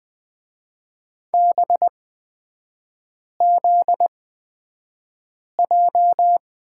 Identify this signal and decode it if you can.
Morse code, heard as a keyed tone, about 20 wpm, spaced as for 5 wpm, 700 Hz. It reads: BZJ
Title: 255 Lesson 41e - Period - Three Characters 20wpm